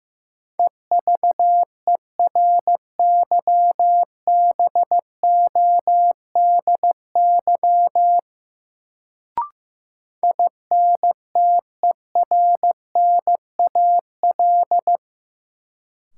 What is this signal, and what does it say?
Morse code, heard as a keyed tone, about 15 wpm, 700 Hz. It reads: EVERYBODY INTERNAL
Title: Sentences from Top 1400 Words - New Word Drill - 15wpm